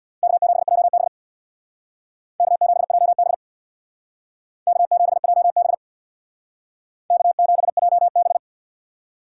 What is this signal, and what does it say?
Morse code, heard as a keyed tone, about 63 wpm, 700 Hz. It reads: K7JB K7JB K7JB K7JB